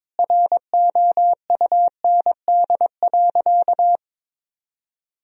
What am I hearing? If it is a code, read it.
Morse code, heard as a keyed tone, about 22 wpm, 700 Hz. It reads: ROUND.
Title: Sentences from Top 1000 Words - Review Part 3 - 22wpm